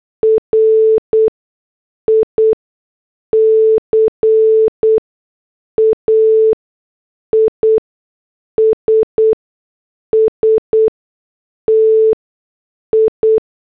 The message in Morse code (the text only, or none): RICA IS STI